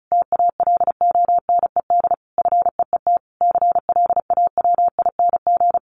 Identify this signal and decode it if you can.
Morse code, heard as a keyed tone, about 34 wpm, 700 Hz. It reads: TALONED FEET CLAWING
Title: A Sound Of Thunder 35wpm